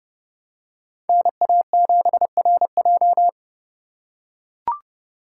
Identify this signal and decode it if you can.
Morse code, heard as a keyed tone, about 30 wpm, 700 Hz. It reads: NA7RJ